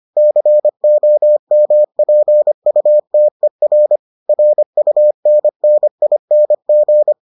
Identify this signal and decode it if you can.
Morse code, heard as a keyed tone, about 25 wpm, 600 Hz. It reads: COMPUTER RUNNING